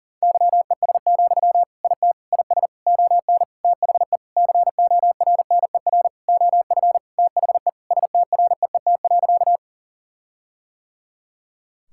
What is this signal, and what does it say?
Morse code, heard as a keyed tone, about 37 wpm, 700 Hz. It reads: YES, IT IS ON THE CORNER OF THE STREET.
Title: Sentences from Top 1600 Words 40wpm